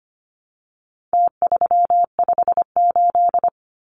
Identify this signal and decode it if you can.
Morse code, heard as a keyed tone, about 25 wpm, 700 Hz. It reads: T358
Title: Numbers - Contest Style - 4-Digit 25wpm